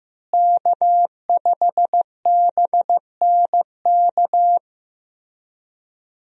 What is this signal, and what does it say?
Morse code, heard as a keyed tone, about 15 wpm, 700 Hz. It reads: K5BNK
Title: USA Call Signs - 5 Characters - Repeated 3x - 15wpm